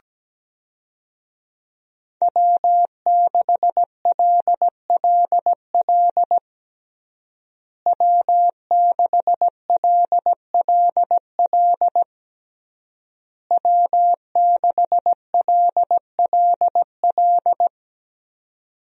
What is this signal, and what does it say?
Morse code, heard as a keyed tone, about 17 wpm, 700 Hz. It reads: W6LLL W6LLL W6LLL
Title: USA Call Signs - 5 Characters - Repeated 3x - 17wpm